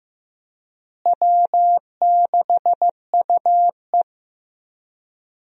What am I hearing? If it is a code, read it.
Morse code, heard as a keyed tone, about 15 wpm, 700 Hz. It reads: W6UE
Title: USA Call Signs - 4 Characters - Repeated 3x - 15wpm